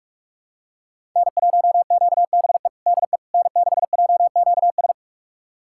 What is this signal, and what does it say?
Morse code, heard as a keyed tone, about 42 wpm, 700 Hz. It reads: N1QBE DE N6JQS
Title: USA Callsign DE Callsign - 5 Characters - 45wpm